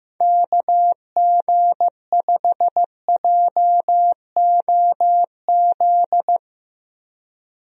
Morse code, heard as a keyed tone, about 15 wpm, 700 Hz. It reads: KG5JOZ
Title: USA Call Signs - 6 Characters - Repeated 2x - 15wpm